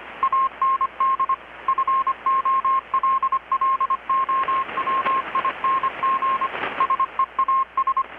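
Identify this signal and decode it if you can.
Morse code, heard as a keyed tone, about 25 wpm, 1070 Hz, with radio noise. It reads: AND FOLLOWING SEAS
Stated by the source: Morse from KPH's Night of Nights XX maritime radiotelegraphy event, received in Chiapas, Mexico about 3,800 km away on a Kenwood R-600 with a grounded 10 m long-wire antenna.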